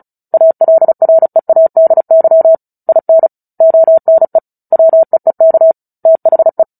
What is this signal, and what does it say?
Morse code, heard as a keyed tone, about 35 wpm, 650 Hz. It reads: ALREADY IN ONE WEEK THE
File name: Book Two / 09-Chapter 9 - 35 WPM